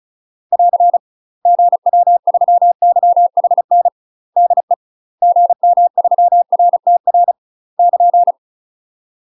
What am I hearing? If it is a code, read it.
Morse code, heard as a keyed tone, about 36 wpm, 700 Hz. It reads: <AR> GW3YHN DE GM3RTR <KN>
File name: UK MASTER QSOs TRAINING/35WPM